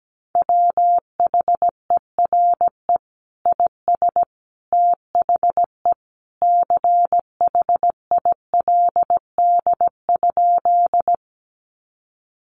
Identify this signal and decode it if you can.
Morse code, heard as a keyed tone, about 17 wpm, 700 Hz. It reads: WHERE IS THE CHILD?
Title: Sentences from Top 800 Words 17wpm